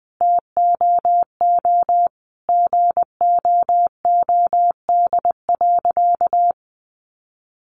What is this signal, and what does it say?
Morse code, heard as a keyed tone, about 20 wpm, 700 Hz. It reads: TOO GOOD.
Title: Sentences from Top 1200 Words 20wpm